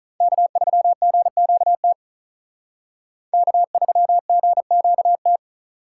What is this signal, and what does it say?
Morse code, heard as a keyed tone, about 43 wpm, 700 Hz. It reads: K3GQT K3GQT
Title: Speed Racing - USA 1x3 Call Signs 35wpm